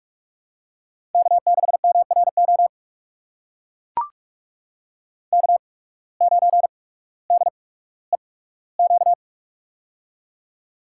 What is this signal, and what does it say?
Morse code, heard as a keyed tone, about 42 wpm, spaced as for 19 wpm, 700 Hz. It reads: K6MRO K9DEQ
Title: USA Call Signs - 8x Character Spacing - 45wpm